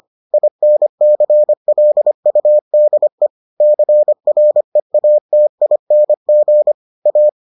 Morse code, heard as a keyed tone, about 25 wpm, 600 Hz. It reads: INCLUDE CREATING A